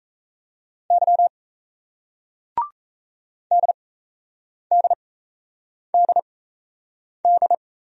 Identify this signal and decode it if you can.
Morse code, heard as a keyed tone, about 40 wpm, spaced as for 8 wpm, 700 Hz. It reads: Y DDDD